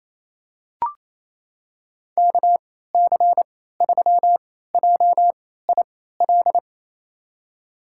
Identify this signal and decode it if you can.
Morse code, heard as a keyed tone, about 28 wpm, spaced as for 16 wpm, 700 Hz. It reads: KC3JIL
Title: USA Call Signs - 6 Characters - 3x Character Spacing - 28wpm